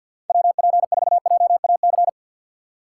Morse code, heard as a keyed tone, about 52 wpm, 690 Hz. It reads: WP4JAC
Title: USA Call Signs - 3x Character Spacing - 50wpm